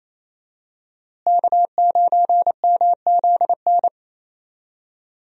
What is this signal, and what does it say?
Morse code, heard as a keyed tone, about 28 wpm, 700 Hz. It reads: K9MZN
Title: USA Call Signs - 5 Characters - Repeated 3x - 28wpm